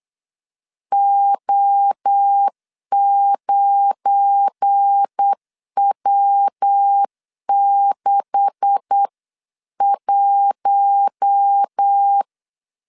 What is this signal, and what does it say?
Morse code, heard as a keyed tone, about 8 wpm, 800 Hz. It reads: O9W61
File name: LEZIONI / 10 SIGNS / 10B INT 8wpm 3534